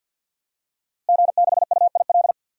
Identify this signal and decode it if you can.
Morse code, heard as a keyed tone, about 50 wpm, 700 Hz. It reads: G6UIL
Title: DXCC-100 Call Signs - 5 Characters - Repeated 3x - 50wpm